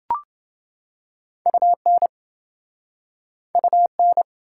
Morse code, heard as a keyed tone, about 28 wpm, 700 Hz. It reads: UN UN